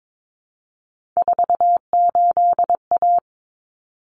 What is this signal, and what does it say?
Morse code, heard as a keyed tone, about 22 wpm, 700 Hz. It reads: 48A
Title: Common Cut Numbers - 3-Digit 22wpm